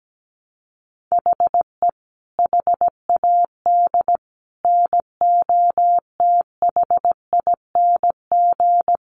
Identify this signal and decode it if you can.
Morse code, heard as a keyed tone, about 17 wpm, 700 Hz. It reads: HE HAD NOTHING
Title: Sentences from Top 1100 Words 17wpm